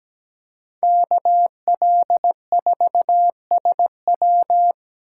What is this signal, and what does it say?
Morse code, heard as a keyed tone, about 20 wpm, 700 Hz. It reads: KL4SW
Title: Speed Racing - USA 2x2 Call Signs 17wpm